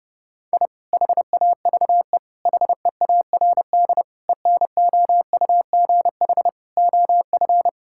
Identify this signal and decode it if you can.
Morse code, heard as a keyed tone, about 30 wpm, 700 Hz. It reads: I HAVE HEARD ENOUGH OF
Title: Sentences from Top 1000 Words - Review Part 6 - 30wpm